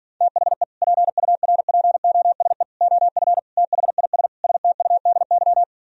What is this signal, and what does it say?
Morse code, heard as a keyed tone, about 51 wpm, 700 Hz. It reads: THE PURPOSE OF THIS STUDY